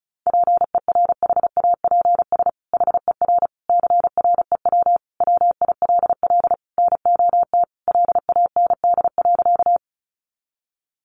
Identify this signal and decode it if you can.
Morse code, heard as a keyed tone, about 34 wpm, 700 Hz. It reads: PERHAPS HER CREW WILL NOT LAND.
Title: Sentences from Top 1500 Words 35wpm